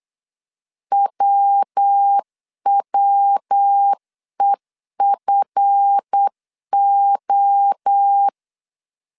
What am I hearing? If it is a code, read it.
Morse code, heard as a keyed tone, about 8 wpm, 800 Hz. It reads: WWEFO
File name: CW / DET / 1 9 1500C S A 8wpm 5637